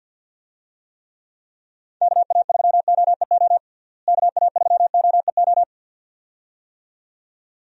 Morse code, heard as a keyed tone, about 51 wpm, 700 Hz. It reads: KA3OEO KA3OEO
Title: USA Call Signs - 6 Characters - Repeated 2x - 50wpm